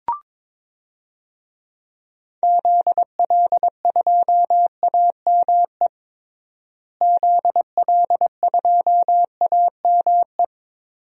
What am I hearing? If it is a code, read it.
Morse code, heard as a keyed tone, about 21 wpm, 700 Hz. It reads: ZL2AME ZL2AME
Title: DXCC-100 Call Signs - 6 Characters - Repeated 2x - 22wpm